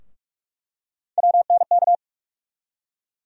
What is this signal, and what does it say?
Morse code, heard as a keyed tone, about 46 wpm, 700 Hz. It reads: WNK
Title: USA Call Sign Postfixes - 3 Letters 45wpm